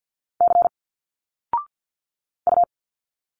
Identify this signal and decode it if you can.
Morse code, heard as a keyed tone, about 46 wpm, 700 Hz. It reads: C U
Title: Single Letters 50wpm